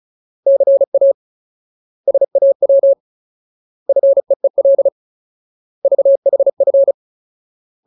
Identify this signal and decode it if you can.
Morse code, heard as a keyed tone, about 35 wpm, 550 Hz. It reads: CA SAW FEEL VHF
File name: Ultimate Word List / 35WPM Ultimate